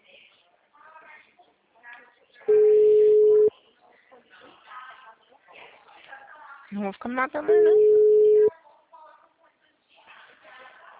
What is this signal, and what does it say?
Ringback tone (Europe, 425 Hz), with speech.